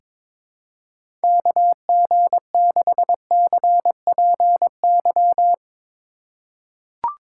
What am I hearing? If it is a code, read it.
Morse code, heard as a keyed tone, about 22 wpm, 700 Hz. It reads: KG6CPY
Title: USA Call Signs - 6 Characters - 4x Character Spacing - 22wpm